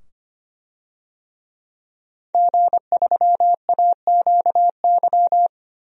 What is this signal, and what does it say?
Morse code, heard as a keyed tone, about 25 wpm, 700 Hz. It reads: G3AQY
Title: DXCC-100 Call Signs - 5 Characters - 25wpm